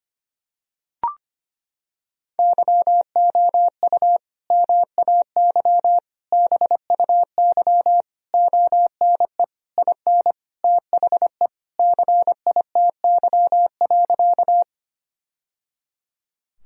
Morse code, heard as a keyed tone, about 25 wpm, 700 Hz. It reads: YOU MAY BUY ONE IN THE CITY.
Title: Sentences from Top 1000 Words - Review Part 4 - 25wpm